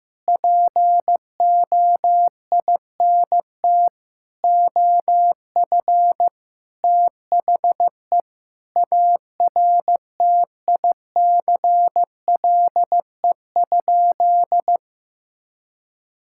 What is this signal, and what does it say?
Morse code, heard as a keyed tone, about 16 wpm, 700 Hz. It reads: POINT OF THE ARTICLE?
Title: Sentences from Top 1000 Words - Review Part 5 - 15wpm